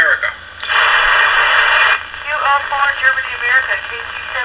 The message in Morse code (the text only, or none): none